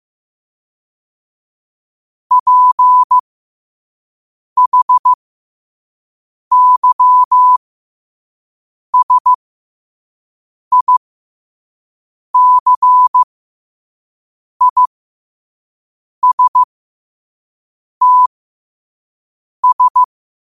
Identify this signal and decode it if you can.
Morse code, heard as a keyed tone, about 15 wpm, spaced as for 5 wpm, 1000 Hz. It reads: PHYSICISTS
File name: qotdfiles / 5wpm